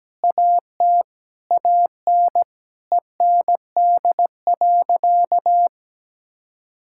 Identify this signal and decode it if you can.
Morse code, heard as a keyed tone, about 17 wpm, 700 Hz. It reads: AT AN END.
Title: Sentences from Top 1600 Words 17wpm